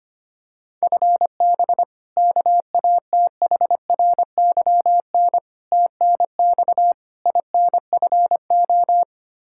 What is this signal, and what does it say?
Morse code, heard as a keyed tone, about 25 wpm, 700 Hz. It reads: FB KATHRYN TNX INFO
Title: Practice QSO - Exchange 3 - More Info - 25wpm